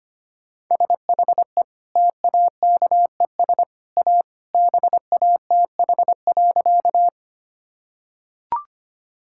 Morse code, heard as a keyed tone, about 25 wpm, 690 Hz. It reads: SHE TAKES A BATH.
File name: Easy Sentences with Hints - 25wpm